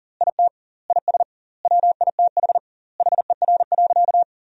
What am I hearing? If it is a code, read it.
Morse code, heard as a keyed tone, about 40 wpm, 700 Hz. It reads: IT IS WITH HER.
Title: 2x Word-Spacing - Sentences from Top 100 Words - 40wpm